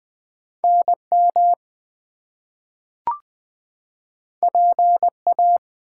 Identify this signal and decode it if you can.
Morse code, heard as a keyed tone, about 20 wpm, 700 Hz. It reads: NM PA
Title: US States and Canadian Provinces 20wpm